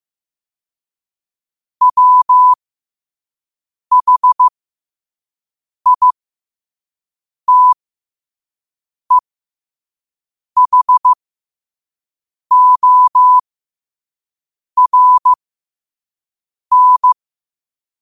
Morse code, heard as a keyed tone, about 15 wpm, spaced as for 5 wpm, 1000 Hz. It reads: WHITEHORN
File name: qotdfiles / 5wpm